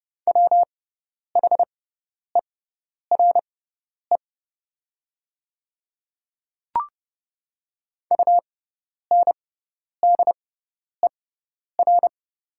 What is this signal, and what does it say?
Morse code, heard as a keyed tone, about 30 wpm, spaced as for 10 wpm, 700 Hz. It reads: WHERE UNDER
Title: 6x Character Spacing - Top 500 Words - 30wpm